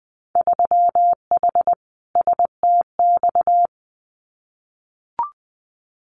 Morse code, heard as a keyed tone, about 20 wpm, 700 Hz. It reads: 3H STX
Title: Winter Field Day Exchanges Encore 20wpm